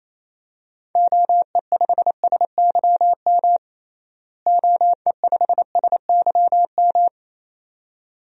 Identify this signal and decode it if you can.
Morse code, heard as a keyed tone, about 28 wpm, 700 Hz. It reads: OE5SYM OE5SYM
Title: DXCC-100 Call Signs - 6 Characters - Repeated 2x - 28wpm